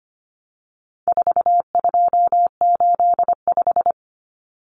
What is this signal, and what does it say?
Morse code, heard as a keyed tone, about 25 wpm, 700 Hz. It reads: 4285
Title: Common Cut Numbers - 4-Digit 25wpm